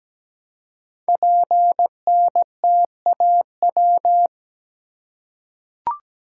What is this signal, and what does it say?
Morse code, heard as a keyed tone, about 17 wpm, 700 Hz. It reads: PNTAW
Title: Code Groups - 5 Letters - 17wpm